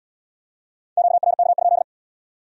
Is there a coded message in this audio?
Yes: Morse code from 1.0 to 1.8 s, heard as a keyed tone, about 73 wpm, 700 Hz. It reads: QRK?